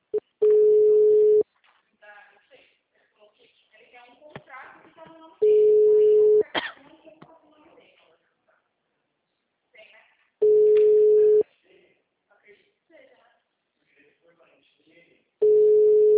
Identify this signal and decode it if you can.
Ringback tone (Europe, 425 Hz), with speech.
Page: Sumare - SP